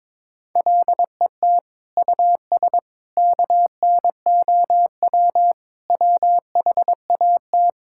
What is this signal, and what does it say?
Morse code, heard as a keyed tone, about 21 wpm, 700 Hz. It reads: LET US KNOW WHAT
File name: practice / 22wpm / Sentences with 8-Word Phrases - 22wpm